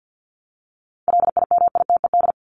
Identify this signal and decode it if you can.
Morse code, heard as a keyed tone, about 50 wpm, 700 Hz. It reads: LIMITED
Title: Top 751-1000 Words 50wpm